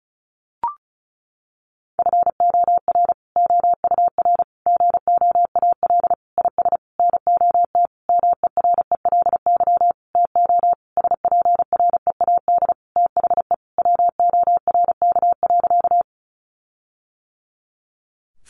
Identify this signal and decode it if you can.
Morse code, heard as a keyed tone, about 34 wpm, 700 Hz. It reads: FOR OUR GOAL IS NOT MERELY TO SPREAD THE WORK.
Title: Sentences from Top 1400 Words 35wpm